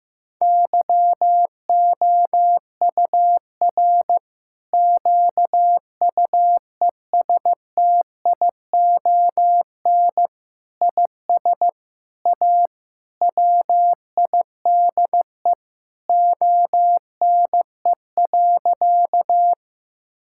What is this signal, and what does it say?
Morse code, heard as a keyed tone, about 16 wpm, 700 Hz. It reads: YOUR QUESTION IS A WIDE ONE.
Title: Sentences from Top 1000 Words - Review Part 6 - 15wpm